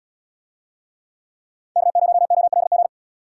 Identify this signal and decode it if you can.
Morse code, heard as a keyed tone, about 75 wpm, 700 Hz. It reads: K0OLG